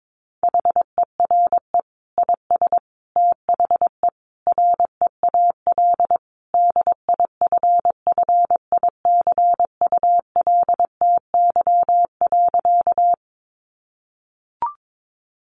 Morse code, heard as a keyed tone, about 22 wpm, 700 Hz. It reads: HERE IS THE REAL DIFFICULTY.
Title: Sentences from Top 1500 Words 22wpm